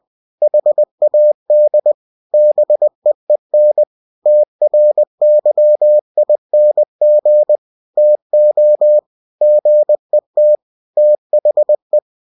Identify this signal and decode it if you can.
Morse code, heard as a keyed tone, about 20 wpm, 600 Hz. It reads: HAD BEEN TRYING TO GET THE
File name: cw-books / winnie-the-pooh / winnie-20wpm-page-0042